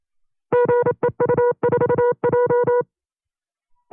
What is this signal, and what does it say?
Morse code, heard as a keyed tone, about 27 wpm, 490 Hz. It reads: GEU4J
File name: CW-practice/25WPM